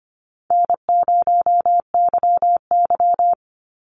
Morse code, heard as a keyed tone, about 25 wpm, 700 Hz. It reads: N0YY
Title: Easy Call Signs Repeated 2x - 25wpm